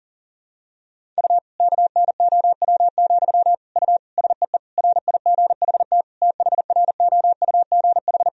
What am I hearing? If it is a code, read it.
Morse code, heard as a keyed tone, about 40 wpm, 700 Hz. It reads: U KNOW, U SEE RIGHT THROUGH
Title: Sentences from Top 200 Words and Abbreviations 40wpm